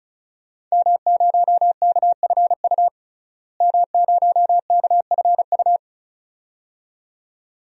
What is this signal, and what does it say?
Morse code, heard as a keyed tone, about 34 wpm, 700 Hz. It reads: M0KFU M0KFU
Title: DXCC-100 Call Signs - 5 Characters - Repeated 2x - 35wpm